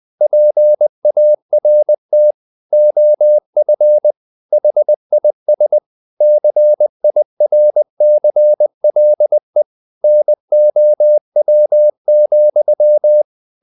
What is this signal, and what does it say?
Morse code, heard as a keyed tone, about 20 wpm, 600 Hz. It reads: PART OF HIS CIRCLE NOW,
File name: the old man and the sea part 2 20wpm